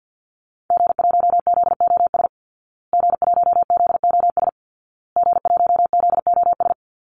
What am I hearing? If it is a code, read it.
Morse code, heard as a keyed tone, about 48 wpm, 700 Hz. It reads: G1ZOS G1ZOS G1ZOS
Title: Call Signs Repeated 2x - 50wpm